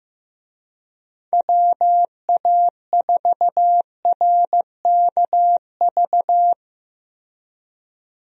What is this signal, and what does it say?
Morse code, heard as a keyed tone, about 15 wpm, 700 Hz. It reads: WA4RKV
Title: USA Call Signs - 6 Characters - Repeated 3x - 15wpm